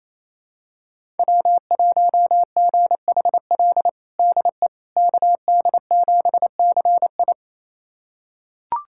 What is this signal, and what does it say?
Morse code, heard as a keyed tone, about 28 wpm, 700 Hz. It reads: W1GHL DE KD7CI
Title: USA Callsign DE Callsign - 5 Characters - 28wpm